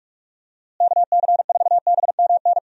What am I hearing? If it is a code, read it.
Morse code, heard as a keyed tone, about 46 wpm, 700 Hz. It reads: KC4BMN
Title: USA Call Signs - 6 Characters - Repeated 3x - 45wpm